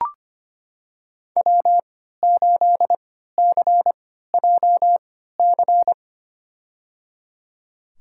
Morse code, heard as a keyed tone, about 25 wpm, spaced as for 14 wpm, 700 Hz. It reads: W8CJC
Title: USA Call Signs - 5 Characters - 3x Character Spacing - 25wpm